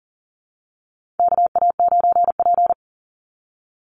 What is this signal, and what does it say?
Morse code, heard as a keyed tone, about 40 wpm, 700 Hz. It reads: KA9P
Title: Call Signs Repeated 2x - 40wpm